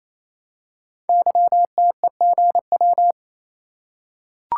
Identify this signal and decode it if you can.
Morse code, heard as a keyed tone, about 27 wpm, 700 Hz. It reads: YTEGW
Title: Code Groups - 5 Letter:Numbers - 28wpm